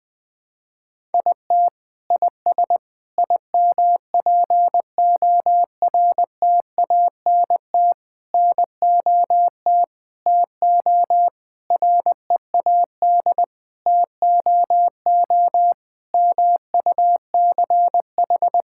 Morse code, heard as a keyed tone, about 20 wpm, 700 Hz. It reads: IT IS IMPORTANT NOT TO READ TOO MUCH
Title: Sentences with 9-Word Phrases - Spoken Beginning - 20wpm